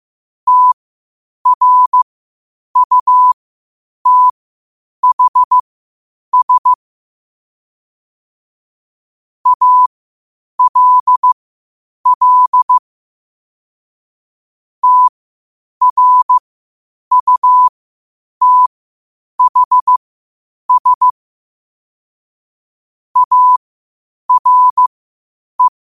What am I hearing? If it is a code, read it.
Morse code, heard as a keyed tone, about 15 wpm, spaced as for 8 wpm, 1000 Hz. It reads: TRUTHS ALL TRUTHS ARE
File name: qotdfiles / 8wpm